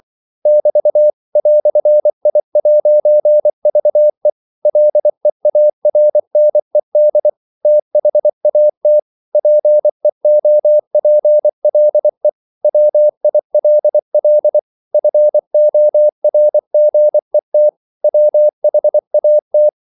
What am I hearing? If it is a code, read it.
Morse code, heard as a keyed tone, about 24 wpm, 600 Hz. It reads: <BT> "I'VE LEARNED THAT PEOPLE WILL FORGET WHAT
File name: CW / Practice / EnglishQuotations0224